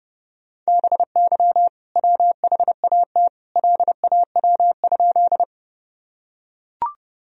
Morse code, heard as a keyed tone, about 30 wpm, 700 Hz. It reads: BY WHAT LAW?
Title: Sentences from Top 1000 Words - Review Part 3 - 30wpm